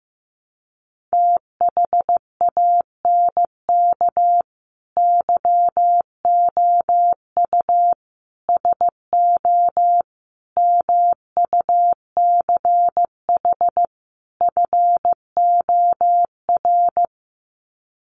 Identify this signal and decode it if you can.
Morse code, heard as a keyed tone, about 15 wpm, 700 Hz. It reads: THANK YOU SO MUCH FOR
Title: Sets of 5 Words 15wpm